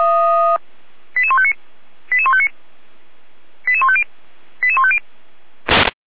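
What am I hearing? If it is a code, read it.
Five-tone selective call (ZVEI), heard as quick runs of five tones, tones of about 70 ms. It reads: 88159 88159 88150 88150
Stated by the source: ZVEI five-tone selective call 88271 received by SDR.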